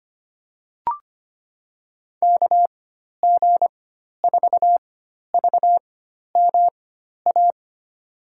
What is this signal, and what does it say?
Morse code, heard as a keyed tone, about 25 wpm, spaced as for 12 wpm, 700 Hz. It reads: KG4VMA
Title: USA Call Signs - 6 Characters - 4x Character Spacing - 25wpm